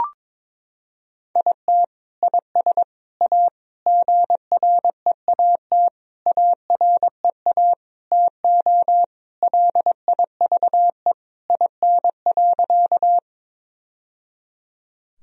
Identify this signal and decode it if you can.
Morse code, heard as a keyed tone, about 22 wpm, 700 Hz. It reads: IT IS A GREAT AREA TO LIVE IN.
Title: Sentences from Top 1000 Words - Review Part 2 - 22wpm